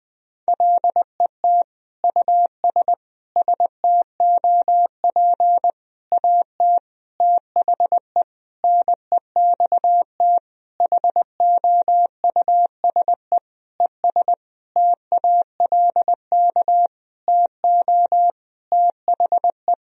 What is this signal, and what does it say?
Morse code, heard as a keyed tone, about 20 wpm, 700 Hz. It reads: LET US STOP AT THE NEXT HOUSE ES TALK TO THE
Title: Sentences from Top 400 Words and Abbreviations 20wpm